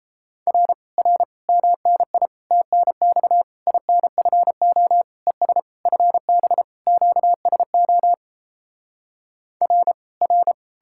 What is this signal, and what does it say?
Morse code, heard as a keyed tone, about 37 wpm, 700 Hz. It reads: R R MNI TNX INFO ES FB QSO R R